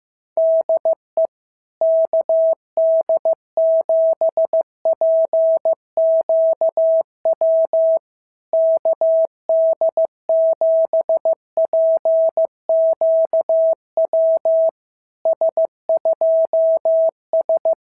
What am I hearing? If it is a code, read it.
Morse code, heard as a keyed tone, about 14 wpm, 650 Hz. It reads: DE KD7PQW KD7PQW S2S
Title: SOTA QSO Practice 15wpm